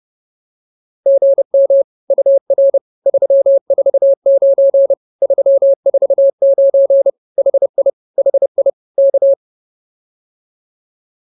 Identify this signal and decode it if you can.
Morse code, heard as a keyed tone, about 28 wpm, 560 Hz. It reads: GM UR 349 349 HI HI K